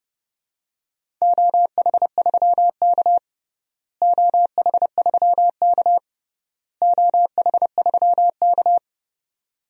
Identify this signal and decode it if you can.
Morse code, heard as a keyed tone, about 32 wpm, 700 Hz. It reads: OH3K OH3K OH3K
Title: DXCC-100 Call Signs - 4 Characters - Repeated 3x - 30wpm